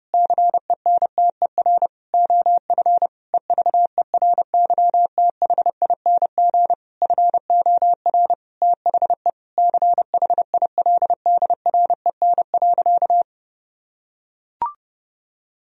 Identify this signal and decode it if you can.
Morse code, heard as a keyed tone, about 30 wpm, 700 Hz. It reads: CENTER OF EVERYTHING FOR THE CHILDREN.
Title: Sentences from Top 1000 Words - Review Part 2 - 30wpm